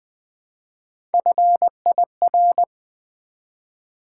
Morse code, heard as a keyed tone, about 20 wpm, 700 Hz. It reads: FIR
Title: Build A Word - 1 Letter At A Time - 4-Letter Words - 20wpm